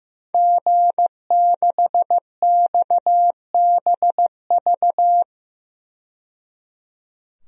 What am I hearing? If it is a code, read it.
Morse code, heard as a keyed tone, about 15 wpm, 700 Hz. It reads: G6XBV
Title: DXCC-100 Call Signs - 5 Characters - 15wpm